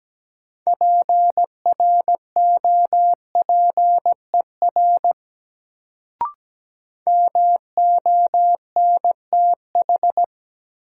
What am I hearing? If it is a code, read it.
Morse code, heard as a keyed tone, about 17 wpm, 700 Hz. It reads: PROPER MONTH
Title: Sentences from Top 800 Words - New Word Drill - 17wpm